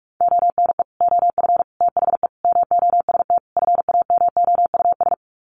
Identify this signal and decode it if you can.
Morse code, heard as a keyed tone, about 44 wpm, 700 Hz. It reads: ONE OF THE MOST FAMOUS
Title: Sets of 5 Words Encore2 45wpm